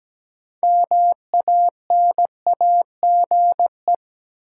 Morse code, heard as a keyed tone, about 17 wpm, 700 Hz. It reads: MANAGE
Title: Sentences from Top 1400 Words - New Word Drill - 17wpm